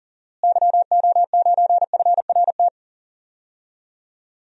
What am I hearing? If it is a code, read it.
Morse code, heard as a keyed tone, about 40 wpm, 700 Hz. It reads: YO9FRT
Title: DXCC-100 Call Signs - 6 Characters - Repeated 3x - 40wpm